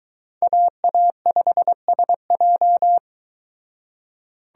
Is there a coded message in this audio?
Yes: Morse code from 0.4 to 3.0 s, heard as a keyed tone, about 22 wpm, 700 Hz. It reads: AA5SJ